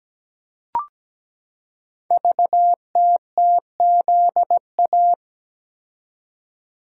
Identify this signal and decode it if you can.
Morse code, heard as a keyed tone, about 17 wpm, 700 Hz. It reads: VTTZA
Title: Code Groups - 5 Letters - 17wpm